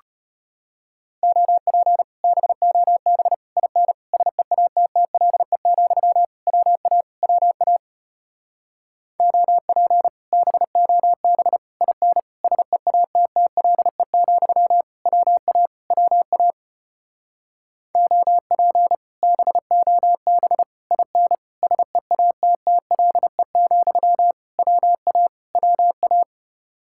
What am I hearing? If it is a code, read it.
Morse code, heard as a keyed tone, about 33 wpm, 700 Hz. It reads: OP BOB IN SEATTLE, WA WA OP BOB IN SEATTLE, WA WA OP BOB IN SEATTLE, WA WA